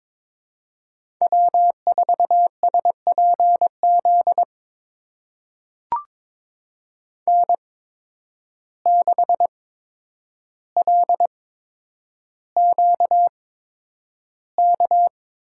Morse code, heard as a keyed tone, about 22 wpm, spaced as for 9 wpm, 700 Hz. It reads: W4SPZ N6LQK
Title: USA Call Signs - 5 Characters - 8x Character Spacing - 22wpm